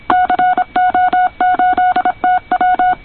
Morse code, heard as a keyed tone, about 26 wpm, 720 Hz. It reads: CO8TW
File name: co8tw / cqwwcw2010